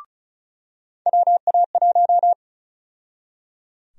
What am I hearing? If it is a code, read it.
Morse code, heard as a keyed tone, about 34 wpm, 700 Hz. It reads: WA1
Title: USA Call Sign Prefixes - 35wpm